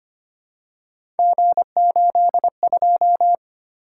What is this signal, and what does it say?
Morse code, heard as a keyed tone, about 25 wpm, 700 Hz. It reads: G82
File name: Three Letter-Number Combinations - 25wpm